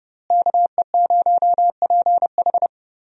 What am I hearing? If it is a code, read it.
Morse code, heard as a keyed tone, about 32 wpm, 700 Hz. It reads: KE0PH
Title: USA Call Signs - 5 Characters - Repeated 3x - 30wpm